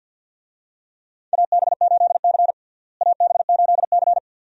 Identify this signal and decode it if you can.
Morse code, heard as a keyed tone, about 51 wpm, 700 Hz. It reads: AB8C AB8C
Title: USA Call Signs - 4 Characters - Repeated 2x - 50wpm